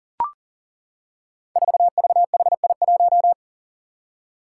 Morse code, heard as a keyed tone, about 38 wpm, 700 Hz. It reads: 4VHI1
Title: Five Challenging Letter-Number Combinations 40wpm